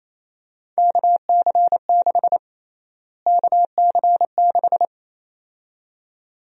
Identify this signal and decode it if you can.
Morse code, heard as a keyed tone, about 28 wpm, 700 Hz. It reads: KC6 KC6
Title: USA Call Sign Prefixes - Repeated 2x - 28wpm